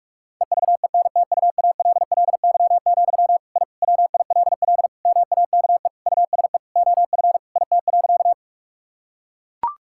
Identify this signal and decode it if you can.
Morse code, heard as a keyed tone, about 41 wpm, 700 Hz. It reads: EVENTUALLY, I WILL MAKE USE OF IT.
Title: Sentences from Top 1000 Words - Review Part 6 - 45wpm